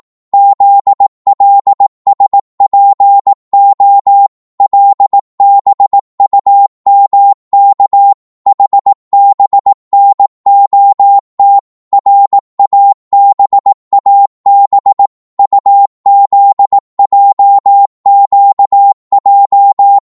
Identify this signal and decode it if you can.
Morse code, heard as a keyed tone, about 18 wpm, 800 Hz. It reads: ZLSPO LBUMK HBNOT RABAB UZJQJ
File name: Morse / ltr-18wpm